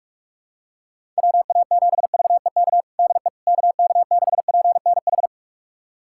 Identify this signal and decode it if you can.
Morse code, heard as a keyed tone, about 41 wpm, 690 Hz. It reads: WA7VEK DE KK6PNH
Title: USA Callsign DE Callsign - 6 Characters - 45wpm